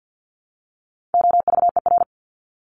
Morse code, heard as a keyed tone, about 48 wpm, 700 Hz. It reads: OVER
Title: Top 200 Words 50wpm